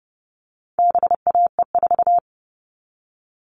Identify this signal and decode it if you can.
Morse code, heard as a keyed tone, about 30 wpm, 700 Hz. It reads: BAE4
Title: All Cut Numbers - 4-Digit 30wpm